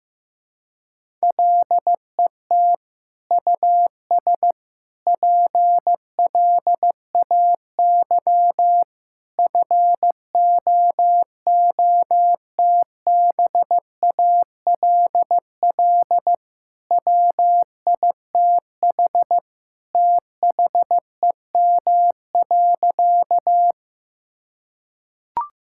Morse code, heard as a keyed tone, about 15 wpm, 700 Hz. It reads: LET US PLAY FOOTBALL WITH THEM.
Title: Sentences from Top 1600 Words 15wpm